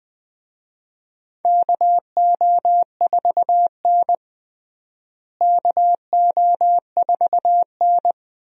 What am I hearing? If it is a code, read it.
Morse code, heard as a keyed tone, about 20 wpm, 700 Hz. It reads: KO4N KO4N
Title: USA Call Signs - 4 Characters - Repeated 3x - 20wpm